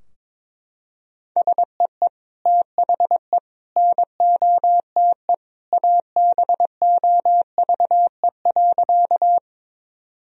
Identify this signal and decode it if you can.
Morse code, heard as a keyed tone, about 22 wpm, 700 Hz. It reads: SEE THE NOTE ABOVE.
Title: Sentences from Top 1000 Words - Review Part 2 - 22wpm